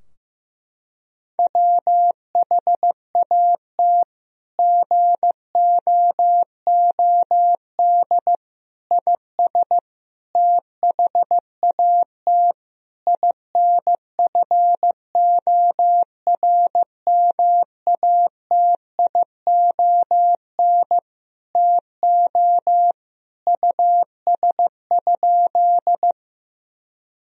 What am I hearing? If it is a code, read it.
Morse code, heard as a keyed tone, about 15 wpm, 700 Hz. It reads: WHAT GOOD IS THAT INFORMATION TO US?
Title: Sentences from Top 1000 Words - Review Part 3 - 15wpm